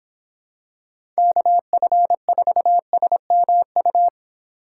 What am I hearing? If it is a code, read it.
Morse code, heard as a keyed tone, about 26 wpm, 700 Hz. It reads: KF4SMU
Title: Speed Racing - USA 2x3 Call Signs 22wpm